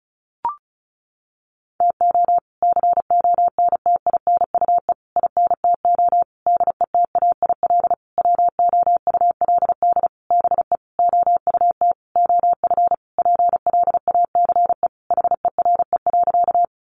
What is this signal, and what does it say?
Morse code, heard as a keyed tone, about 34 wpm, 700 Hz. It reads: TO CONTINUE INTO DETAIL WOULD BE OUT OF PLACE HERE.
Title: Sentences from Top 1500 Words 35wpm